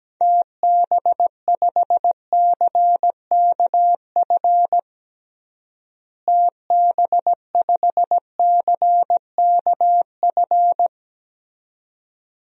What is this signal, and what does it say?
Morse code, heard as a keyed tone, about 17 wpm, 700 Hz. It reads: TB5CKF TB5CKF
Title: DXCC-100 Call Signs - 6 Characters - Repeated 2x - 17wpm